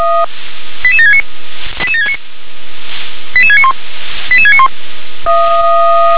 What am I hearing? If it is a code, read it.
Five-tone selective call (ZVEI), heard as quick runs of five tones, tones of about 70 ms. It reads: 88769 88769 88691 88691